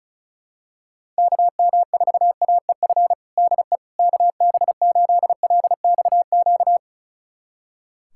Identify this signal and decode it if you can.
Morse code, heard as a keyed tone, about 36 wpm, 700 Hz. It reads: KM4AEF DE KB8LXQ
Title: USA Callsign DE Callsign - 6 Characters - 35wpm